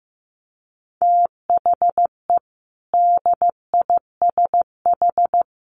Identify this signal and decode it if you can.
Morse code, heard as a keyed tone, about 15 wpm, 700 Hz. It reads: THE DISH
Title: Sentences from Top 1400 Words 15wpm